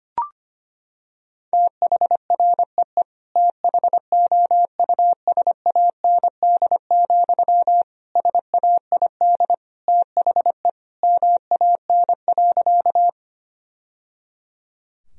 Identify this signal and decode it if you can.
Morse code, heard as a keyed tone, about 25 wpm, 700 Hz. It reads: THREE THOUSAND, SAID THE MAN.
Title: Sentences from Top 1000 Words - Review Part 3 - 25wpm